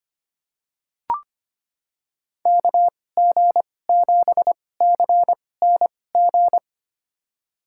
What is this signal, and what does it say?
Morse code, heard as a keyed tone, about 25 wpm, spaced as for 18 wpm, 700 Hz. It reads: KG7CNG
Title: USA Call Signs - 2x Character Spacing - 25wpm